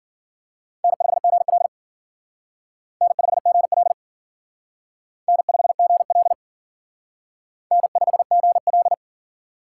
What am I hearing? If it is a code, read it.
Morse code, heard as a keyed tone, about 49 wpm, 700 Hz. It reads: N5GL N5GL N5GL N5GL
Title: Speed Racing - USA Call Signs 40wpm